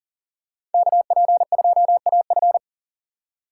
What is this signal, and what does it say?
Morse code, heard as a keyed tone, about 39 wpm, 700 Hz. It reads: KP2AF
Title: USA Call Signs - 5 Characters - 3x Character Spacing - 40wpm